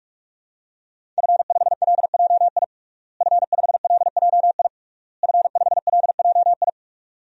Morse code, heard as a keyed tone, about 46 wpm, 700 Hz. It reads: F5LJI F5LJI F5LJI
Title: DXCC-100 Call Signs - 5 Characters - Repeated 3x - 45wpm